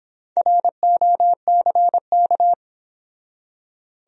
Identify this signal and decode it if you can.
Morse code, heard as a keyed tone, about 26 wpm, 700 Hz. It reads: ROCK